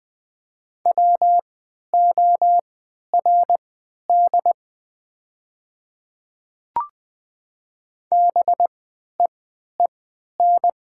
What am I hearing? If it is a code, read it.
Morse code, heard as a keyed tone, about 20 wpm, spaced as for 11 wpm, 700 Hz. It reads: WORD BEEN
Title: 3x Character Spacing - Top 200 Words - 20wpm